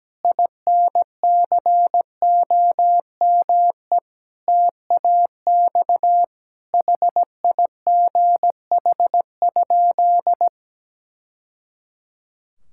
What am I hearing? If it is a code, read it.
Morse code, heard as a keyed tone, about 17 wpm, 700 Hz. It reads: INCOME TAX HIGH?
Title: Sentences from Top 1000 Words - Review Part 5 - 17wpm